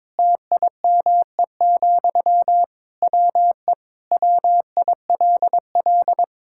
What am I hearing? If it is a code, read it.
Morse code, heard as a keyed tone, about 22 wpm, 700 Hz. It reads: TIME, WE WILL
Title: Sentences from Top 1000 Words - Review Part 5 - 22wpm